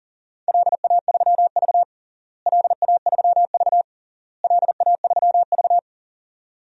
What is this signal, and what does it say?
Morse code, heard as a keyed tone, about 37 wpm, 700 Hz. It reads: LA3V LA3V LA3V
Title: DXCC-100 Call Signs - 4 Characters - Repeated 3x - 40wpm